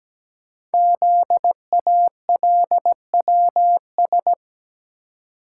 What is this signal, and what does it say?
Morse code, heard as a keyed tone, about 17 wpm, 700 Hz. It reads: ZALWS
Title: Code Groups - 5 Letters - 17wpm